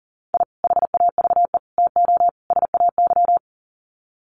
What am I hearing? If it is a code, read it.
Morse code, heard as a keyed tone, about 40 wpm, 700 Hz. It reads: I HAVE TO SAY
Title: Sets of 4 Words 40wpm